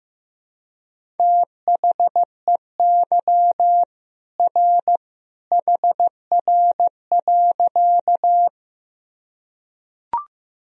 Morse code, heard as a keyed tone, about 15 wpm, 700 Hz. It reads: THEY R HR.
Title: Sentences from Top 200 Words and Abbreviations 15wpm